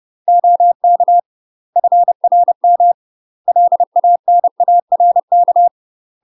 Morse code, heard as a keyed tone, about 30 wpm, 700 Hz. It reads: OK FRM LANARK
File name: UK MASTER QSOs TRAINING / 30WPM / UK MASTER QSOs 0039